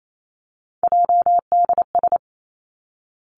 Morse code, heard as a keyed tone, about 28 wpm, 700 Hz. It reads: JDS
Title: Three Challenging Letter-Number Combinations 28wpm